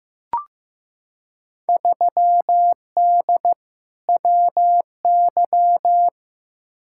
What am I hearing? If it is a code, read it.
Morse code, heard as a keyed tone, about 15 wpm, 700 Hz. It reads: 3D WY
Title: Field Day Exchanges Encore - 15wpm